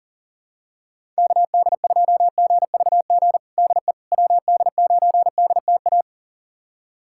Morse code, heard as a keyed tone, about 37 wpm, 700 Hz. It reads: KD2GVG DE WD9DTA
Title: USA Callsign DE Callsign - 6 Characters - 40wpm